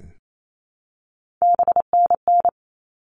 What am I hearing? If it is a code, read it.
Morse code, heard as a keyed tone, about 28 wpm, 700 Hz. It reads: BNN
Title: All Cut Numbers - 3-Digit 28wpm